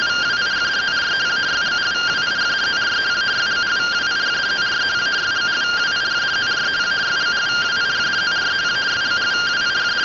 Two-tone teleprinter signal, about 170 Hz shift.
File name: findings / 81 axial iodine 262.gen-43 rtty rx